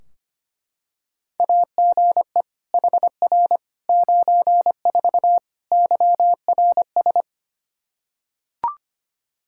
Morse code, heard as a keyed tone, about 25 wpm, 700 Hz. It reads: AGE HR 94 YRS
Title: Practice QSO - Exchange 3 - More Info - 25wpm